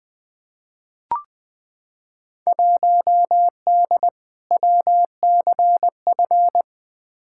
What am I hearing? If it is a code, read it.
Morse code, heard as a keyed tone, about 20 wpm, 700 Hz. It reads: 1D WCF
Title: Field Day Exchanges - 20wpm